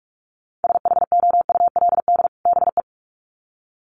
Morse code, heard as a keyed tone, about 44 wpm, 700 Hz. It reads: SHOULD BE
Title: Sets of 2 Words 45wpm